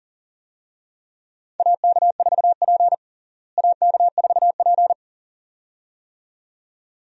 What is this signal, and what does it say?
Morse code, heard as a keyed tone, about 40 wpm, 700 Hz. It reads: AK4P AK4P
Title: USA Call Signs - 4 Characters - Repeated 2x - 40wpm